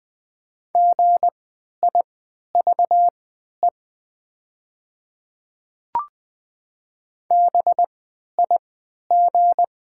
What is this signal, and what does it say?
Morse code, heard as a keyed tone, about 20 wpm, spaced as for 11 wpm, 700 Hz. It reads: GIVE BIG
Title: 3x Character Spacing - Top 200 Words - 20wpm